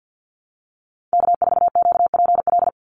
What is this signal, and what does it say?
Morse code, heard as a keyed tone, about 48 wpm, 700 Hz. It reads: K4QPL
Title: Call Signs Repeated 2x - 50wpm